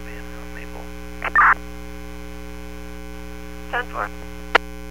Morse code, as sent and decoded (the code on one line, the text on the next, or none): none